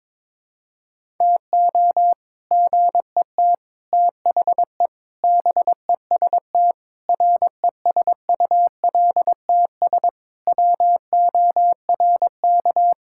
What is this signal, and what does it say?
Morse code, heard as a keyed tone, about 21 wpm, 700 Hz. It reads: TO GET THE BEST RESULTS WORK
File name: practice / 22wpm / Sentences with 3-Word Phrases - 22wpm